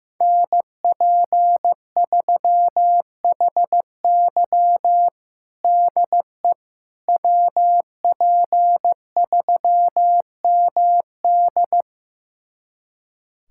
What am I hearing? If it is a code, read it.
Morse code, heard as a keyed tone, about 15 wpm, 700 Hz. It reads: NP3HY DE WP3MD
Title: USA Callsign DE Callsign - 5 Characters - 15wpm